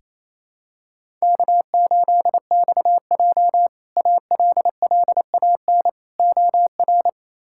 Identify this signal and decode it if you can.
Morse code, heard as a keyed tone, about 28 wpm, 700 Hz. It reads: K8XJ ALLAN OR
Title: K1USN SST QSO Practice - 28wpm